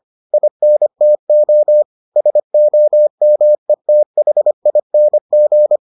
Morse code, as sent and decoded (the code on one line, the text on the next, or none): .. -. - --- / ... --- -- . - .... .. -. --.
INTO SOMETHING